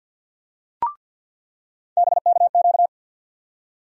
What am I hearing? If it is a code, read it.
Morse code, heard as a keyed tone, about 48 wpm, 700 Hz. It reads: BKQ
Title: USA Call Sign Postfixes - 3 Letters 50wpm